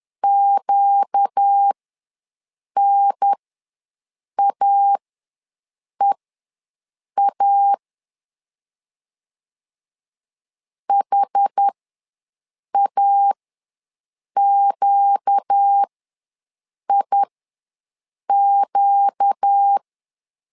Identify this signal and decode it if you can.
Morse code, heard as a keyed tone, about 10 wpm, spaced as for 6 wpm, 800 Hz. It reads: QNAEA HAQIQ
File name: LEZIONI / 4 Q ws 10wpm 2306